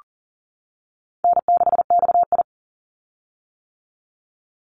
Morse code, heard as a keyed tone, about 40 wpm, 700 Hz. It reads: N6XI
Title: Easy Call Signs - 40wpm